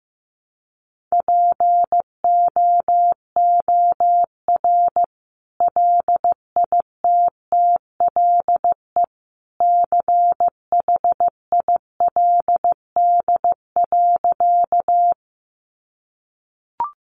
Morse code, heard as a keyed tone, about 15 wpm, 700 Hz. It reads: POOR LITTLE CHILD.
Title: Sentences from Top 1000 Words 15wpm